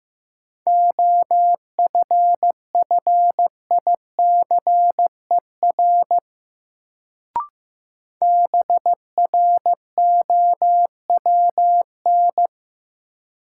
Morse code, heard as a keyed tone, about 15 wpm, 700 Hz. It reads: OFFICER BROWN
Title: Sentences from Top 1200 Words - New Word Drill - 15wpm